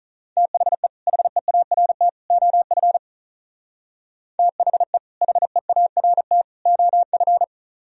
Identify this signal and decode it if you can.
Morse code, heard as a keyed tone, about 44 wpm, 700 Hz. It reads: THE HEART OF THE HEART OF